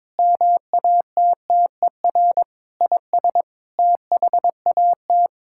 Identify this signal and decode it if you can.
Morse code, heard as a keyed tone, about 22 wpm, 700 Hz. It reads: MATTER IS THAT
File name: practice / 22wpm / Sets of 8 Words - 22wpm